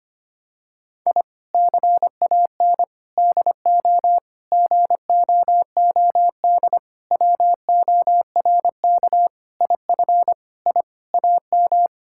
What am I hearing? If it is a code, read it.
Morse code, heard as a keyed tone, about 25 wpm, 700 Hz. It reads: I CAN DO GOOD WORK IF I AM
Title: Sentences from Top 1000 Words - Review Part 6 - 25wpm